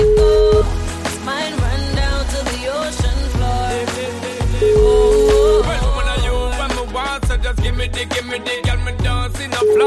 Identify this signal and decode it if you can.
Ringback tone (Europe, 425 Hz), with noise.